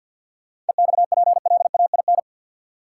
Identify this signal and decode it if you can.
Morse code, heard as a keyed tone, about 53 wpm, 700 Hz. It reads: EXPLAIN
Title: Sentences from Top 1100 Words - New Word Drill Alt - 50wpm